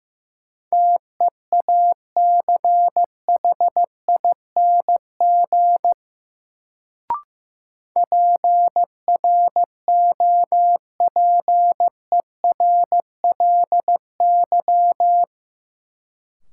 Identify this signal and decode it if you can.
Morse code, heard as a keyed tone, about 15 wpm, 700 Hz. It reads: TEACHING PROPERLY
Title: Sentences from Top 1100 Words - New Word Drill - 15wpm